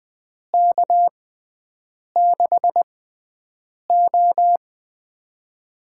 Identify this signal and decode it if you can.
Morse code, heard as a keyed tone, about 20 wpm, spaced as for 7 wpm, 700 Hz. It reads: K6O
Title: USA Call Signs - 4 Characters - 6x Character Spacing - 20wpm